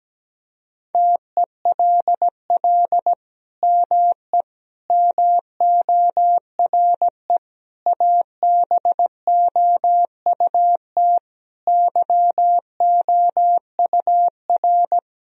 Morse code, heard as a keyed tone, about 17 wpm, 700 Hz. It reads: TELL ME MORE ABOUT YOUR
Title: Sentences from Top 1000 Words - Review Part 4 - 17wpm